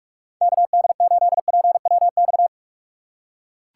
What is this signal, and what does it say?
Morse code, heard as a keyed tone, about 44 wpm, 700 Hz. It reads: KD8PWX
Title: USA Call Signs - 6 Characters - 6x Character Spacing - 45wpm